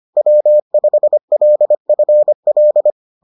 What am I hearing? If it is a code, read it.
Morse code, heard as a keyed tone, about 25 wpm, 600 Hz. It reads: W5LFL